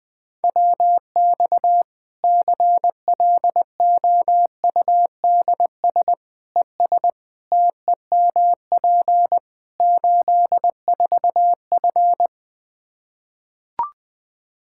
Morse code, heard as a keyed tone, about 20 wpm, 700 Hz. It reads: WX CLOUDS ES TEMP 84F
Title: Practice QSO - Exchange 2 - Info - 20wpm